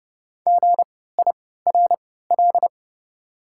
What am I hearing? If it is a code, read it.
Morse code, heard as a keyed tone, about 29 wpm, spaced as for 17 wpm, 690 Hz. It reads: GIRL
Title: 3x Character Spacing - Top 500 Words - 30wpm